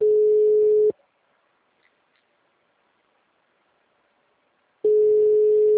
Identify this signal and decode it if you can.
Ringback tone (Europe, 425 Hz).